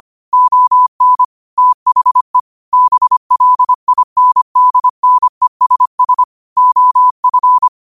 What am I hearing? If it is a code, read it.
Morse code, heard as a keyed tone, about 25 wpm, 1000 Hz. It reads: ON THE BLINDNESS OF